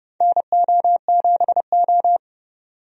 Morse code, heard as a keyed tone, about 30 wpm, 700 Hz. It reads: NO7O
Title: USA Call Signs - 4 Characters - 30wpm